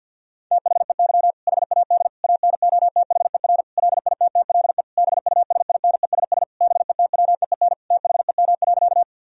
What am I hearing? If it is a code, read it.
Morse code, heard as a keyed tone, about 50 wpm, 700 Hz. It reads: THEY HAD ANOTHER LITTLE BUSINESS BETWEEN THEM.
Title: Sentences from Top 1000 Words - Review Part 1 - 50wpm